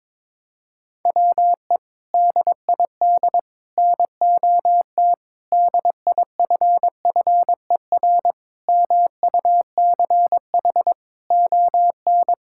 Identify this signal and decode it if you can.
Morse code, heard as a keyed tone, about 22 wpm, 700 Hz. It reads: WE DID NOT DIFFER MUCH ON
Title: Sentences from Top 1000 Words - Review Part 6 - 22wpm